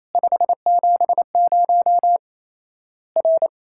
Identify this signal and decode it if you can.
Morse code, heard as a keyed tone, about 28 wpm, 700 Hz. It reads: 570 R
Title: Worked All Europe DX Contest - QTC Exchange - 28wpm